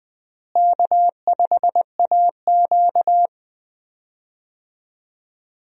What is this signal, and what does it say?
Morse code, heard as a keyed tone, about 20 wpm, 700 Hz. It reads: K5AQ
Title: USA Call Signs - 4 Characters - Repeated 2x - 20wpm